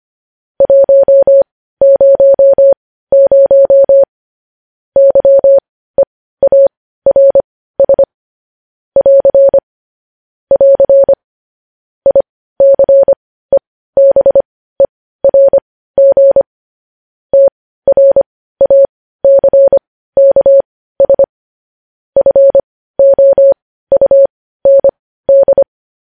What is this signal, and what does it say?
Morse code, heard as a keyed tone, about 25 wpm, spaced as for 15 wpm, 550 Hz. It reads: 100 YEARS <AR> <AR> ICEBERG TRACKS FOUND
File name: podcast / News headlines-1x-25x15